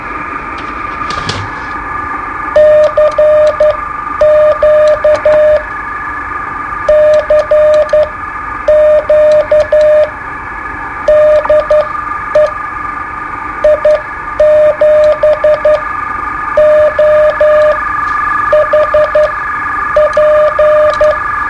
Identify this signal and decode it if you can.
Morse code, heard as a keyed tone, about 11 wpm, spaced as for 8 wpm, 610 Hz, with radio noise. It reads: CQ CQ DE I7OHP
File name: qso-01